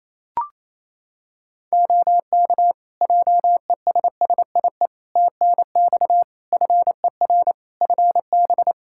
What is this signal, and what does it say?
Morse code, heard as a keyed tone, about 27 wpm, 700 Hz. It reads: OK JESSIE TNX FER FB
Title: Practice QSO - Exchange 4 - Ending - 28wpm